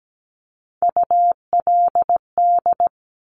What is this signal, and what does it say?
Morse code, heard as a keyed tone, about 17 wpm, 700 Hz. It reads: ULD
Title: Top Three Letter Combinations in Words 17wpm